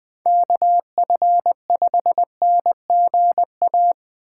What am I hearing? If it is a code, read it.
Morse code, heard as a keyed tone, about 20 wpm, 700 Hz. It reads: KF5NGA